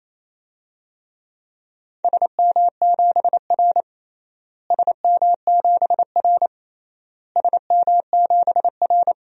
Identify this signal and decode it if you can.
Morse code, heard as a keyed tone, about 28 wpm, 700 Hz. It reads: SM7R SM7R SM7R
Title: DXCC-100 Call Signs - 4 Characters - Repeated 3x - 28wpm